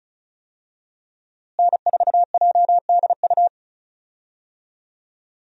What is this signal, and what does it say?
Morse code, heard as a keyed tone, about 34 wpm, 700 Hz. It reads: N4JDU
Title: USA Call Signs - 5 Characters - Repeated 2x - 35wpm